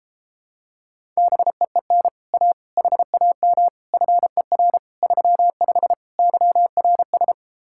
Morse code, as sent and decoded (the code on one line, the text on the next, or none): -... . -. / .- / .... .- -- / ..-. .-. / ...-- ..... / -.-- .-. ...
BEEN A HAM FER 35 YRS